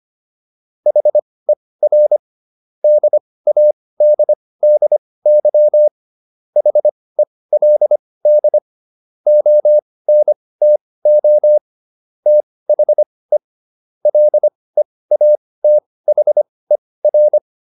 Morse code, heard as a keyed tone, about 25 wpm, spaced as for 18 wpm, 600 Hz. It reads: HER DADDY HELD ONTO THE LEATHER
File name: SS123 18